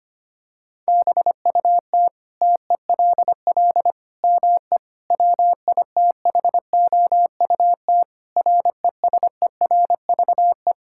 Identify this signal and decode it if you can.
Morse code, heard as a keyed tone, about 25 wpm, 700 Hz. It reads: BUT TELL ME WITHOUT RESERVE
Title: Sentences from Top 1000 Words - Review Part 6 - 25wpm